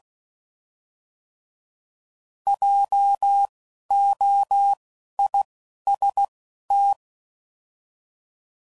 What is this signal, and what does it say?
Morse code, heard as a keyed tone, about 16 wpm, spaced as for 11 wpm, 800 Hz. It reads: JOIST